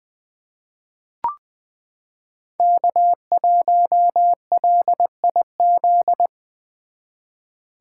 Morse code, heard as a keyed tone, about 20 wpm, 700 Hz. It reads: K1LIZ
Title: 2023 SOTA Chaser Call Signs - 20wpm